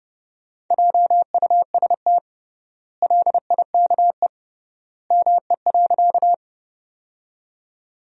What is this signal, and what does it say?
Morse code, heard as a keyed tone, about 30 wpm, 700 Hz. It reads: JUST LIKE ME.
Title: 3x Word-Spacing - Sentences from Top 200 Words - 30wpm